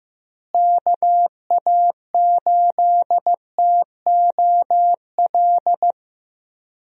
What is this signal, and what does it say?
Morse code, heard as a keyed tone, about 15 wpm, 700 Hz. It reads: KA8TOL
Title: USA Call Signs - 6 Characters - Repeated 3x - 15wpm